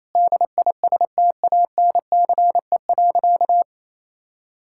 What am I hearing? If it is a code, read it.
Morse code, heard as a keyed tone, about 28 wpm, 700 Hz. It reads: DISTANCE.
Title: Sentences from Top 1600 Words 28wpm